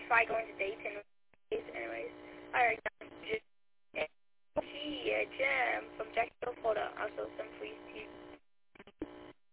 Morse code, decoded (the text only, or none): none